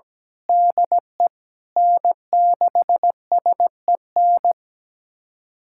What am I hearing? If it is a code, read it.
Morse code, heard as a keyed tone, about 17 wpm, 700 Hz. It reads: DE N6SEN
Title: USA Callsign DE Callsign - 5 Characters - 17wpm